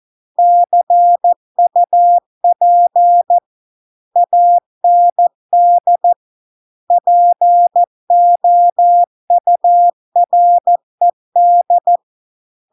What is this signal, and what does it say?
Morse code, heard as a keyed tone, about 14 wpm, 700 Hz. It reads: CUP AND POURED